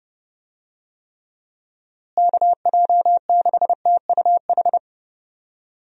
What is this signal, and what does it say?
Morse code, heard as a keyed tone, about 30 wpm, 700 Hz. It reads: KJ6TUH
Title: USA Call Signs - 6 Characters - Repeated 3x - 30wpm